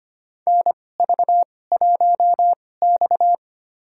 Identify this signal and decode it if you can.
Morse code, heard as a keyed tone, about 25 wpm, spaced as for 18 wpm, 700 Hz. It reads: NV1X
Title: USA Call Signs - 4 Characters - 2x Character Spacing - 25wpm